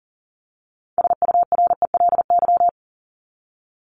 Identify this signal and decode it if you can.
Morse code, heard as a keyed tone, about 40 wpm, 700 Hz. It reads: SURELY
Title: LY Suffix - Top 100 Words with Base Words 40wpm